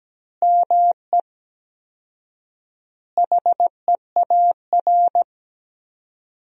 Morse code, heard as a keyed tone, about 17 wpm, 700 Hz. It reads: ME HEAR
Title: 4x Word-Spacing - Sentences from Top 300 Words - 17wpm